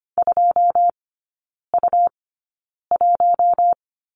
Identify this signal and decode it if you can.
Morse code, heard as a keyed tone, about 25 wpm, spaced as for 9 wpm, 700 Hz. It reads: 2U1
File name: Lesson 17e - Letter u - Three Characters 25wpm